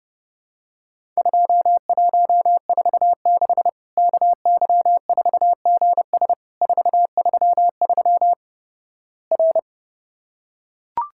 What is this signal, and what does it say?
Morse code, heard as a keyed tone, about 32 wpm, 700 Hz. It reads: 2146 KY4GS 433 R
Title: Worked All Europe DX Contest - QTC Exchange - 30wpm